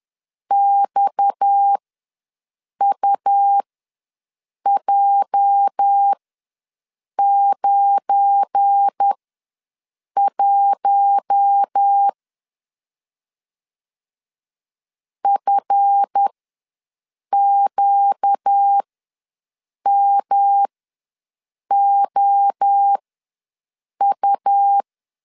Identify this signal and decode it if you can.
Morse code, heard as a keyed tone, about 10 wpm, spaced as for 6 wpm, 800 Hz. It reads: XUJ91 FQMOU
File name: LEZIONI / 8 YJ45 / 8 YJ ws 10wpm 4043